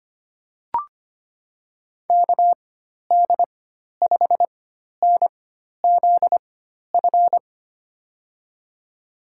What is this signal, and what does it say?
Morse code, heard as a keyed tone, about 25 wpm, spaced as for 12 wpm, 700 Hz. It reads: KD5NZF
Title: USA Call Signs - 6 Characters - 4x Character Spacing - 25wpm